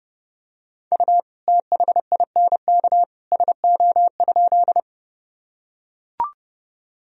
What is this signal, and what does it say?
Morse code, heard as a keyed tone, about 30 wpm, 700 Hz. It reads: U THINK SO?
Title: Sentences from Top 200 Words and Abbreviations 30wpm